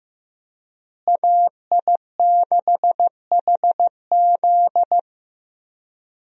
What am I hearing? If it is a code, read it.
Morse code, heard as a keyed tone, about 15 wpm, 700 Hz. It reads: AI6HZ
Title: USA Call Signs - 5 Characters - Repeated 3x - 15wpm